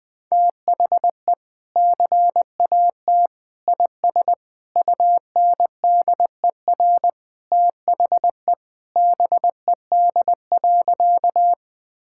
Morse code, heard as a keyed tone, about 20 wpm, 700 Hz. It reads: THE CAT IS UNDER THE BED.
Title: Sentences with 2-Word Phrases - 20wpm